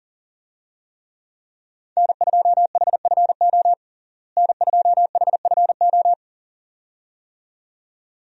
Morse code, heard as a keyed tone, about 37 wpm, 700 Hz. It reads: N2HFO N2HFO
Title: USA Call Signs - 5 Characters - Repeated 2x - 40wpm